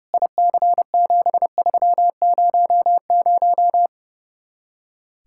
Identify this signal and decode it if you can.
Morse code, heard as a keyed tone, about 34 wpm, 700 Hz. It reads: IC7300
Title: Speed-Racing Common QSO Phrases Encore - 30wpm